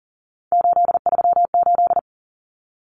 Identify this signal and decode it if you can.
Morse code, heard as a keyed tone, about 40 wpm, 700 Hz. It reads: 838
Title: Numbers - 3-Digit 40wpm